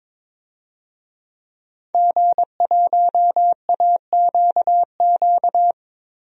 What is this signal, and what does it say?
Morse code, heard as a keyed tone, about 22 wpm, 700 Hz. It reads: G1AQQ
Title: DXCC-100 Call Signs - 5 Characters - 22wpm